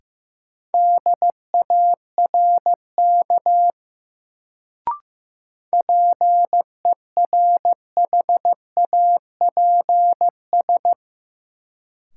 Morse code, heard as a keyed tone, about 15 wpm, 700 Hz. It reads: DARK PERHAPS
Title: Sentences from Top 500 Words - New Word Drill - 15wpm